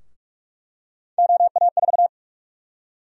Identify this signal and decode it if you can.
Morse code, heard as a keyed tone, about 42 wpm, 700 Hz. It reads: OA4
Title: USA Call Sign Prefixes - 45wpm